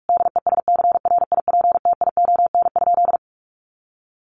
Morse code, heard as a keyed tone, about 44 wpm, 700 Hz. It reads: DESCRIPTION?
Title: Sentences from Top 1500 Words 45wpm